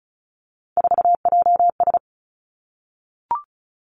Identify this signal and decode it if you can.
Morse code, heard as a keyed tone, about 34 wpm, 700 Hz. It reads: 4JS
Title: Three Challenging Letter-Number Combinations 35wpm